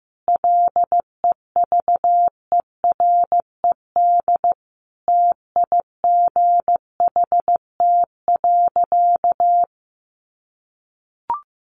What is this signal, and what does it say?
Morse code, heard as a keyed tone, about 15 wpm, 700 Hz. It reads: LEVERED TIGHT.
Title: A Sound Of Thunder 15wpm